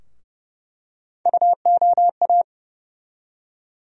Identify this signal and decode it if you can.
Morse code, heard as a keyed tone, about 32 wpm, 700 Hz. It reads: UOA
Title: USA Call Sign Postfixes - 3 Letters 30wpm